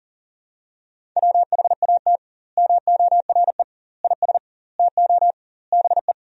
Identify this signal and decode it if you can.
Morse code, heard as a keyed tone, about 40 wpm, 700 Hz. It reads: WHAT MORE IS TO BE
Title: 2x Word-Spacing - Sentences from Top 500 Words - 40wpm